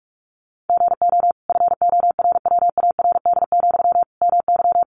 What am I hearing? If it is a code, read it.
Morse code, heard as a keyed tone, about 44 wpm, 700 Hz. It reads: GO FORWARD, MY
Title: Sentences from Top 1300 Words 45wpm